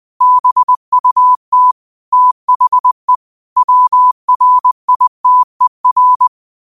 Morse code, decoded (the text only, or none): BUT THE WRITER